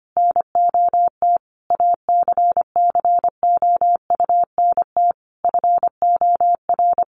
Morse code, heard as a keyed tone, about 25 wpm, 700 Hz. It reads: NOT ACCOUNT FOR
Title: Sentences from Top 900 Words 25wpm